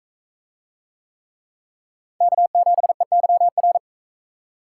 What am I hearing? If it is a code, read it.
Morse code, heard as a keyed tone, about 44 wpm, 700 Hz. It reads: K7EYR